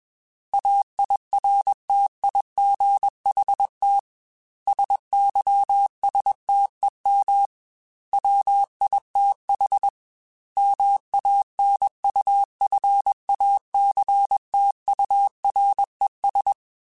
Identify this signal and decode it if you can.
Morse code, heard as a keyed tone, about 21 wpm, 800 Hz. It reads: AIRTIGHT SYSTEM WITH MANUFACTURES